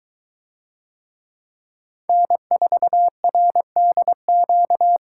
Morse code, heard as a keyed tone, about 22 wpm, 700 Hz. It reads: N4RDQ